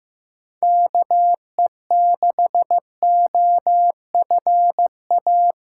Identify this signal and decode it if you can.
Morse code, heard as a keyed tone, about 15 wpm, 700 Hz. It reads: KE6OFA
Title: USA Call Signs - 6 Characters - 6x Character Spacing - 15wpm